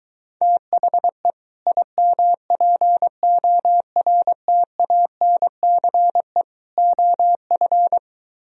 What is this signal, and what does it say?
Morse code, heard as a keyed tone, about 23 wpm, 700 Hz. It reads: THE IMPORTANCE OF